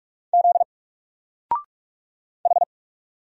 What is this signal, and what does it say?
Morse code, heard as a keyed tone, about 44 wpm, 700 Hz. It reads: Z H
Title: Single Character (Letter:Number:Punctuation:Prosign) 45wpm